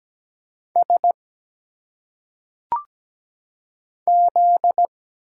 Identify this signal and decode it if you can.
Morse code, heard as a keyed tone, about 17 wpm, 700 Hz. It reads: S Z
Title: Single Letters - Phonetic Pronunciation 17wpm